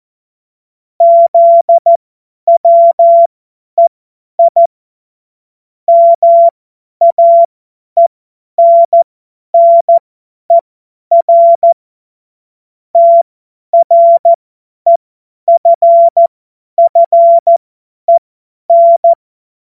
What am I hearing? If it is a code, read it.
Morse code, heard as a keyed tone, about 14 wpm, spaced as for 10 wpm, 680 Hz. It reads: ZWEI MAENNER TREFFEN
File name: Morsekurs / Witze / Witz 22. August 2023 50 F70